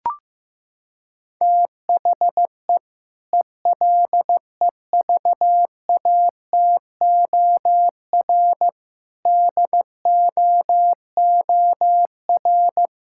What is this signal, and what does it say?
Morse code, heard as a keyed tone, about 15 wpm, 700 Hz. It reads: THE ELEVATOR DOOR
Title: Sentences from Top 1600 Words 15wpm